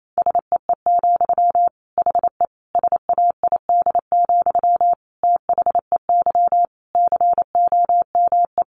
Morse code, heard as a keyed tone, about 28 wpm, 700 Hz. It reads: SEE, HE SAID, THEY COME
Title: Sentences from Top 700 Words 28wpm